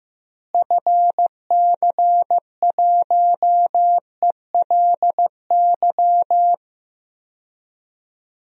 Morse code, heard as a keyed tone, about 15 wpm, 700 Hz. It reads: FC1ELY
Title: DXCC-100 Call Signs - 6 Characters - Repeated 2x - 15wpm